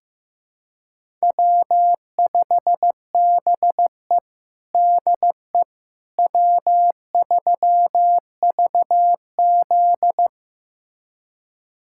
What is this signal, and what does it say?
Morse code, heard as a keyed tone, about 15 wpm, 700 Hz. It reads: W5BE DE W3VZ
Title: USA Callsign DE Callsign - 4 Characters - 15wpm